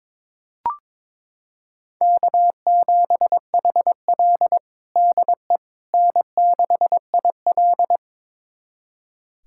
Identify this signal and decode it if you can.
Morse code, heard as a keyed tone, about 22 wpm, 700 Hz. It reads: K7HL DE N6IL
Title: USA Callsign DE Callsign - 4 Characters - 22wpm